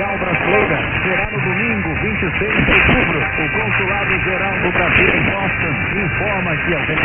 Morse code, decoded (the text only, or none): none